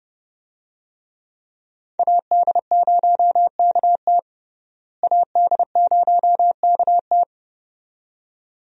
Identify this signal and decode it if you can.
Morse code, heard as a keyed tone, about 32 wpm, 700 Hz. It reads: AD0KT AD0KT
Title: USA Call Signs - 5 Characters - Repeated 2x - 30wpm